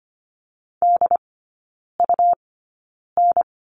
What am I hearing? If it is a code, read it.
Morse code, heard as a keyed tone, about 25 wpm, 700 Hz. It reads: D U N
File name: course / 25:10wpm / 100 Lesson 17e - Letter u - Three Characters 25wpm